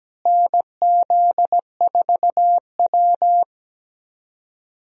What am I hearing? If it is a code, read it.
Morse code, heard as a keyed tone, about 17 wpm, 700 Hz. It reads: NZ4W
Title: USA Call Signs - 4 Characters - Repeated 3x - 17wpm